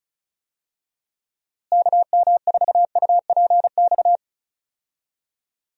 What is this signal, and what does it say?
Morse code, heard as a keyed tone, about 34 wpm, 700 Hz. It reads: KM4UPX
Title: USA Call Signs - 6 Characters - Repeated 2x - 35wpm